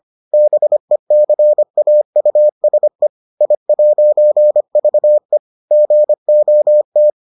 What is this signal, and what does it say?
Morse code, heard as a keyed tone, about 25 wpm, 600 Hz. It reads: BECAUSE I'VE GOT